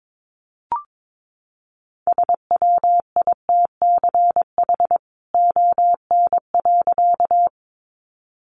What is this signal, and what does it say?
Morse code, heard as a keyed tone, about 22 wpm, 700 Hz. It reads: SWITCH ON.
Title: A Sound Of Thunder 22wpm